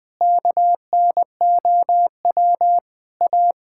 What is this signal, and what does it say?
Morse code, heard as a keyed tone, about 21 wpm, 700 Hz. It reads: KNOW A
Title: Sentences from Top 1000 Words - Review Part 5 - 20wpm